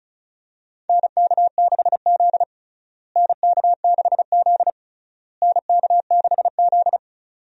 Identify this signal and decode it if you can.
Morse code, heard as a keyed tone, about 36 wpm, 700 Hz. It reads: NK6Z NK6Z NK6Z
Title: USA Call Signs - 4 Characters - Repeated 3x - 35wpm